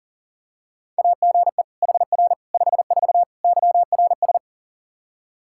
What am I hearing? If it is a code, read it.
Morse code, heard as a keyed tone, about 40 wpm, 700 Hz. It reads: AGE HR 54 YRS
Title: Practice QSO - Exchange 3 - More Info - 40wpm